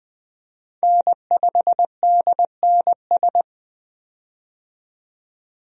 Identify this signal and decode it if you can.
Morse code, heard as a keyed tone, about 20 wpm, 700 Hz. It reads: N5DNS
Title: USA Call Signs - 5 Characters - 20wpm